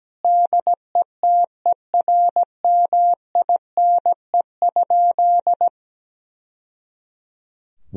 Morse code, heard as a keyed tone, about 17 wpm, 700 Hz. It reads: DETERMINE?
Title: Sentences from Top 1000 Words - Review Part 4 - 17wpm